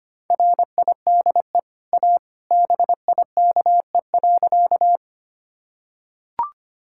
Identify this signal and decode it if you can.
Morse code, heard as a keyed tone, about 25 wpm, 690 Hz. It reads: RIDE A BIKE.
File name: Easy Sentences with Hints - 25wpm